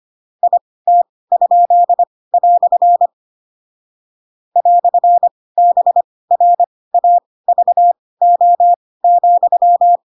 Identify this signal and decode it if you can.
Morse code, heard as a keyed tone, about 25 wpm, spaced as for 18 wpm, 700 Hz. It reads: IT?" "BRAVO,
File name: Pinocchio / Pinocchio ch2 12 25 ik1hge